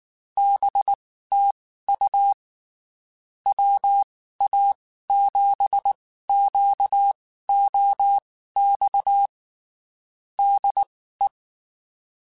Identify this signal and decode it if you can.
Morse code, heard as a keyed tone, about 19 wpm, spaced as for 14 wpm, 800 Hz. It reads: BTU WA7QOX DE
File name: (spaced) / QSOs Spaced-mp3s / WA7QOX 1 @18 sp